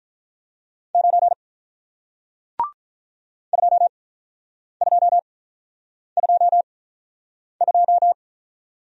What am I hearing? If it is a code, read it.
Morse code, heard as a keyed tone, about 46 wpm, 700 Hz. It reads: 9 2 2 2 2